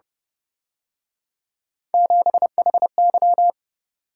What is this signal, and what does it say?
Morse code, heard as a keyed tone, about 29 wpm, 700 Hz. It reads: 7HY
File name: Speed-Racing - Three Letter-Number Combinations - 25wpm